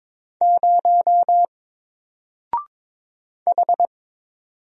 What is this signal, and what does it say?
Morse code, heard as a keyed tone, about 22 wpm, 700 Hz. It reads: 0 H
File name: Single-Letter-Number-22wpm